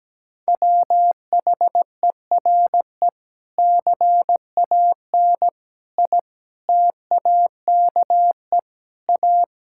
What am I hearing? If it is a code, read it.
Morse code, heard as a keyed tone, about 17 wpm, 700 Hz. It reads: WHERE CAN I TAKE A
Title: Sentences from Top 1600 Words 17wpm